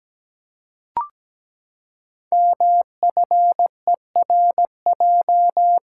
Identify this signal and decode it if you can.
Morse code, heard as a keyed tone, about 17 wpm, 700 Hz. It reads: MFERJ
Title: Code Groups - 5 Letters - 17wpm